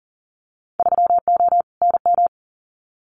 Morse code, heard as a keyed tone, about 42 wpm, 700 Hz. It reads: 3O NM
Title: Winter Field Day Exchanges Encore 40wpm